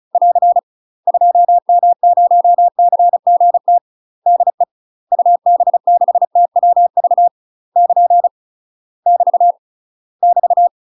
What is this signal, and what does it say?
Morse code, heard as a keyed tone, about 34 wpm, 700 Hz. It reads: <AR> 2M0CGT DE UB6TWV <KN> <BT> <BT>
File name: UK MASTER QSOs TRAINING / 35WPM / UK MASTER QSOs 0005